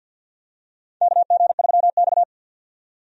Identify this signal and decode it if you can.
Morse code, heard as a keyed tone, about 47 wpm, 700 Hz. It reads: KG3X
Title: USA Call Signs - 4 Characters - 50wpm